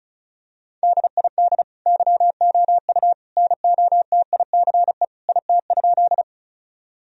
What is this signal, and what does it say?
Morse code, heard as a keyed tone, about 36 wpm, 700 Hz. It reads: DID YOU NOTICE IT?
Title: Sentences from Top 1000 Words - Review Part 2 - 35wpm